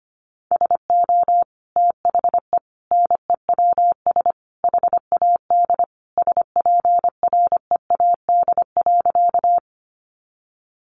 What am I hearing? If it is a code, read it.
Morse code, heard as a keyed tone, about 25 wpm, 700 Hz. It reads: SO THE NEWS HAD SPREAD.
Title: Sentences from Top 1200 Words 25wpm